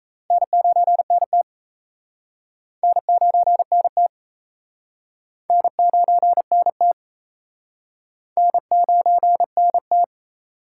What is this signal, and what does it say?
Morse code, heard as a keyed tone, about 37 wpm, 700 Hz. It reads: N9NT N9NT N9NT N9NT